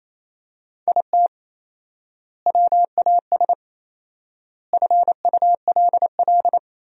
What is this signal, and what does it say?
Morse code, heard as a keyed tone, about 28 wpm, 700 Hz. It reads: IT WAS FULL
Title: 4x Word-Spacing - Sentences from Top 500 Words - 28wpm